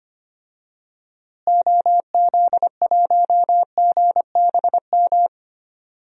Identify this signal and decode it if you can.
Morse code, heard as a keyed tone, about 25 wpm, 700 Hz. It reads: OZ1GBM
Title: DXCC-100 Call Signs - 6 Characters - Repeated 3x - 25wpm